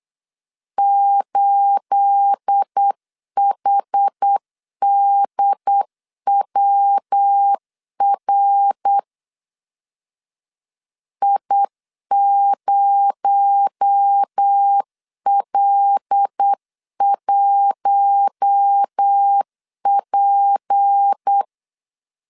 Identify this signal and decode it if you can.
Morse code, heard as a keyed tone, about 8 wpm, 800 Hz. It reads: 8HDWR I0L1P
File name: CW / DET / 1 9 1500C S A 8wpm 5637